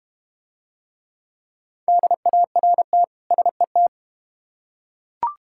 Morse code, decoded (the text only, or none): DART SET